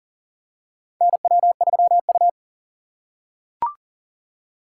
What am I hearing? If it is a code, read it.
Morse code, heard as a keyed tone, about 40 wpm, 700 Hz. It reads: NW3U
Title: USA Call Signs - 4 Characters - 4x Character Spacing - 40wpm